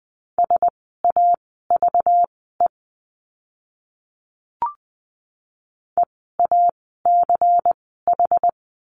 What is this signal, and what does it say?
Morse code, heard as a keyed tone, about 20 wpm, spaced as for 15 wpm, 700 Hz. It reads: SAVE EACH
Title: 192 Lesson 31f - Letter v - Words 20wpm